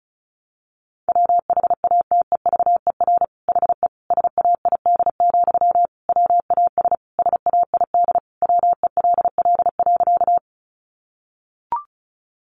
Morse code, heard as a keyed tone, about 34 wpm, 700 Hz. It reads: WHATEVER HE SAID, WAS SAID WELL.
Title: Sentences from Top 1100 Words 35wpm